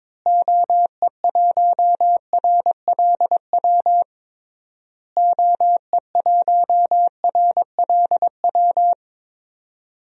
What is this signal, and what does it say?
Morse code, heard as a keyed tone, about 22 wpm, 700 Hz. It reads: OE1RLW OE1RLW
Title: DXCC-100 Call Signs - 6 Characters - Repeated 3x - 22wpm